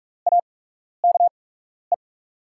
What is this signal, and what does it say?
Morse code, heard as a keyed tone, about 44 wpm, spaced as for 12 wpm, 690 Hz. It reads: AKE
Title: 8x Character Spacing - Top 300 Words - 45wpm